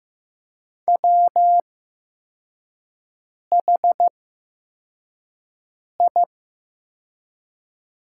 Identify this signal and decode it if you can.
Morse code, heard as a keyed tone, about 15 wpm, spaced as for 4 wpm, 700 Hz. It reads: WHI
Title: 8x Character Spacing - Top 300 Words - 15wpm